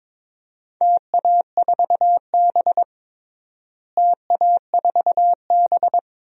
Morse code, heard as a keyed tone, about 21 wpm, 700 Hz. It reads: TA4B TA4B
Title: DXCC-100 Call Signs - 4 Characters - Repeated 2x - 22wpm